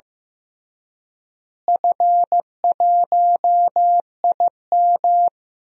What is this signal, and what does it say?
Morse code, heard as a keyed tone, about 16 wpm, 700 Hz. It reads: F1IM
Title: DXCC-100 Call Signs - 4 Characters - Repeated 3x - 15wpm